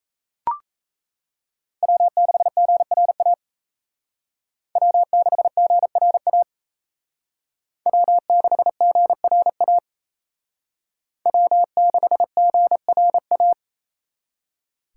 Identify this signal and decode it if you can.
Morse code, heard as a keyed tone, about 35 wpm, 700 Hz. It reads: W6GRA W6GRA W6GRA W6GRA
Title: Speed Racing - USA 1x3 Call Signs 28wpm